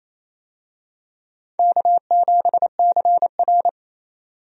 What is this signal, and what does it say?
Morse code, heard as a keyed tone, about 28 wpm, 700 Hz. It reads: K7CR
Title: USA Call Signs - 4 Characters - 28wpm